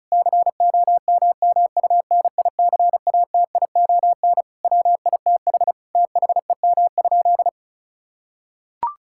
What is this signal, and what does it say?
Morse code, heard as a keyed tone, about 36 wpm, 700 Hz. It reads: COMMUNICATION WITH THEM?
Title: Sentences from Top 1000 Words - Review Part 6 - 35wpm